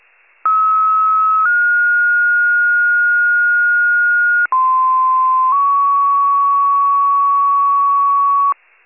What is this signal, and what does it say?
Two-tone page, heard as a one-second tone, then a three-second tone, about 1290 then 1430 Hz, about 1040 then 1120 Hz. It is heard from 0.4 to 8.5 s.
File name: wcalert